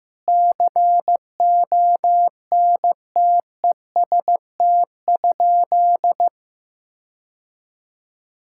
Morse code, heard as a keyed tone, about 15 wpm, 700 Hz. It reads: CONTEST?
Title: Sentences from Top 1600 Words 15wpm